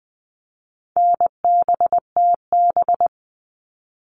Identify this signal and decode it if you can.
Morse code, heard as a keyed tone, about 20 wpm, 700 Hz. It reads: NBTB
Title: All Cut Numbers - 4-Digit 20wpm